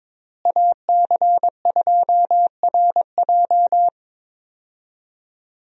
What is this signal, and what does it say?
Morse code, heard as a keyed tone, about 22 wpm, 700 Hz. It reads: AC2RJ
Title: USA Call Signs - 5 Characters - Repeated 3x - 22wpm